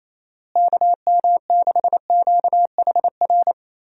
Morse code, heard as a keyed tone, about 28 wpm, 700 Hz. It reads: KM6QHR
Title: USA Call Signs - 6 Characters - Repeated 2x - 28wpm